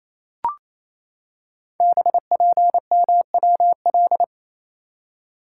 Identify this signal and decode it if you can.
Morse code, heard as a keyed tone, about 28 wpm, 700 Hz. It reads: BPMWL
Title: Code Groups - 5 Letters - 28wpm